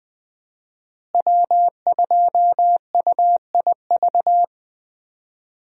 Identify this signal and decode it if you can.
Morse code, heard as a keyed tone, about 21 wpm, 700 Hz. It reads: W2UIV
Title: USA Call Signs - 5 Characters - Repeated 3x - 20wpm